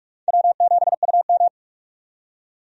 Morse code, heard as a keyed tone, about 42 wpm, 700 Hz. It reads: W7UM
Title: USA Call Signs - 4 Characters - 45wpm